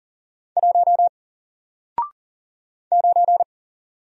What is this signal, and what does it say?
Morse code, heard as a keyed tone, about 40 wpm, 700 Hz. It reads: 1 9
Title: Single Letter-Number Alt 40wpm